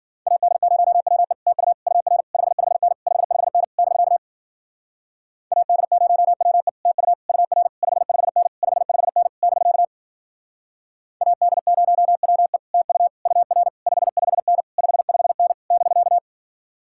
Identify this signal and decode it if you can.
Morse code, heard as a keyed tone, about 53 wpm, 700 Hz. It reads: AD0WE TU UR 55N 55N <BK> AD0WE TU UR 55N 55N <BK> AD0WE TU UR 55N 55N <BK>